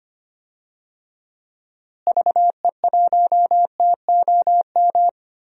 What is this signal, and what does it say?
Morse code, heard as a keyed tone, about 25 wpm, 700 Hz. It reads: VE1TOM
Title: DXCC-100 Call Signs - 6 Characters - Repeated 3x - 25wpm